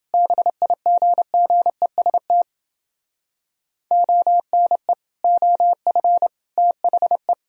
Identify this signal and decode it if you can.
Morse code, heard as a keyed tone, about 29 wpm, 700 Hz. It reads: BIGGEST ONE OF THE